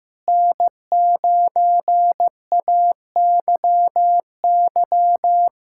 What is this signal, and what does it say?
Morse code, heard as a keyed tone, about 15 wpm, 700 Hz. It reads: N9AYY
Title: USA Callsign DE Callsign - 5 Characters - 15wpm